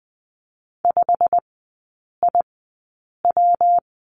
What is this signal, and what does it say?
Morse code, heard as a keyed tone, about 20 wpm, spaced as for 8 wpm, 700 Hz. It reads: 5IW
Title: 127 Lesson 21e - Number 6 - Three Characters 20wpm